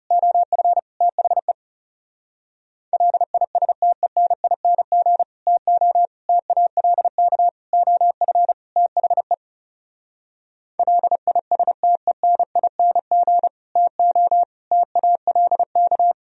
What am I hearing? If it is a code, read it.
Morse code, heard as a keyed tone, about 40 wpm, 700 Hz. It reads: OF THE LISTENING TO TALK OF THE LISTENING TO TALK